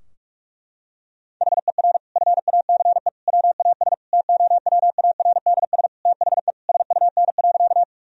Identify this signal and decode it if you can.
Morse code, heard as a keyed tone, about 46 wpm, 700 Hz. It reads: HER FACE WAS TOWARDS THE SUN.
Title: Sentences from Top 1000 Words - Review Part 2 - 45wpm